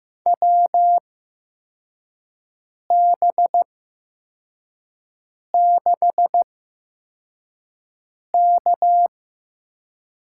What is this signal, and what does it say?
Morse code, heard as a keyed tone, about 15 wpm, spaced as for 4 wpm, 700 Hz. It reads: WB6K
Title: USA Call Signs - 6 Characters - 8x Character Spacing - 15wpm